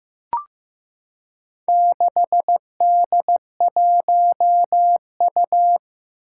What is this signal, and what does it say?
Morse code, heard as a keyed tone, about 15 wpm, 700 Hz. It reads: 6D1U
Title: Four Challenging Letter-Number Combinations 15wpm